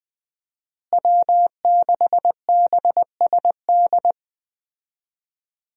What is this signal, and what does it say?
Morse code, heard as a keyed tone, about 20 wpm, 700 Hz. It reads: W6BSD
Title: USA Call Signs - 5 Characters - Repeated 3x - 20wpm